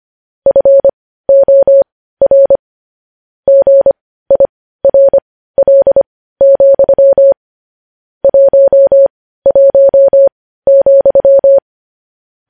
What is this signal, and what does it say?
Morse code, heard as a keyed tone, about 25 wpm, spaced as for 15 wpm, 550 Hz. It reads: FOR GIRL, 11,
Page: news-headlines-in-morse-code-at-15-wpm